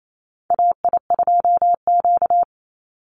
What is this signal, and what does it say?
Morse code, heard as a keyed tone, about 28 wpm, 700 Hz. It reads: AI2Q
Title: Call Signs Repeated 3x - 28wpm